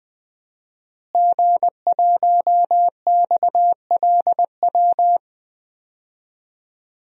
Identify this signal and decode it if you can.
Morse code, heard as a keyed tone, about 21 wpm, 700 Hz. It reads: G1XLW
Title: DXCC-100 Call Signs - 5 Characters - Repeated 2x - 20wpm